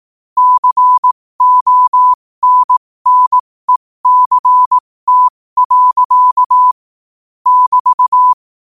Morse code, heard as a keyed tone, about 18 wpm, spaced as for 15 wpm, 1000 Hz. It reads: CONNECT. <BT>
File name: qotdfiles / 15wpm